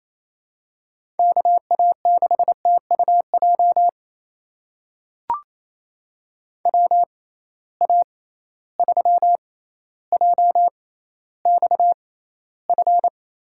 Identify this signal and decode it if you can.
Morse code, heard as a keyed tone, about 28 wpm, spaced as for 14 wpm, 700 Hz. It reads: KA6TUJ WA3JXF
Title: USA Call Signs - 6 Characters - 6x Character Spacing - 28wpm